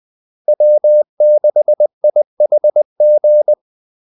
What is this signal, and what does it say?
Morse code, heard as a keyed tone, about 20 wpm, 600 Hz. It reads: W6IHG
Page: Kaldesignaler 20 wpm